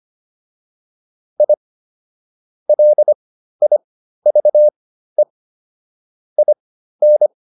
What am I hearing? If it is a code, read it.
Morse code, heard as a keyed tone, about 25 wpm, spaced as for 13 wpm, 600 Hz. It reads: I LIVE IN